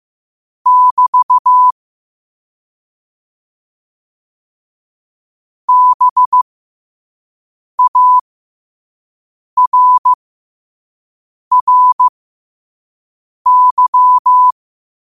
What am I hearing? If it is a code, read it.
Morse code, heard as a keyed tone, about 15 wpm, spaced as for 5 wpm, 1000 Hz. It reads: <BT> BARRY